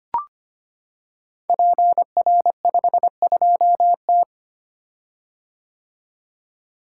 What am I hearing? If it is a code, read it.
Morse code, heard as a keyed tone, about 25 wpm, 700 Hz. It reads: PR52T
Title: Code Groups - 5 Letter:Numbers - 25wpm